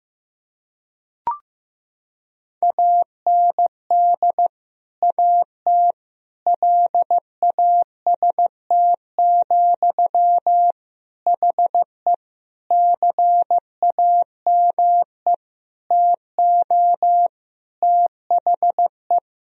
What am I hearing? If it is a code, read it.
Morse code, heard as a keyed tone, about 15 wpm, 700 Hz. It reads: AND AT LAST, HE CAME TO THE
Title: Sentences from Top 1000 Words - Review Part 2 - 15wpm